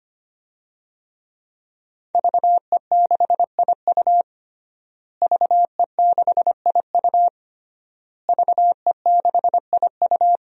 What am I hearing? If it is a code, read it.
Morse code, heard as a keyed tone, about 25 wpm, 700 Hz. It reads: VE6IU VE6IU VE6IU
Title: DXCC-100 Call Signs - 5 Characters - Repeated 3x - 25wpm